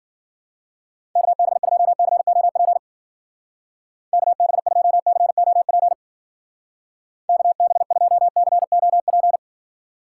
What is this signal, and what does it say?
Morse code, heard as a keyed tone, about 50 wpm, 700 Hz. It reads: KB2COP KB2COP KB2COP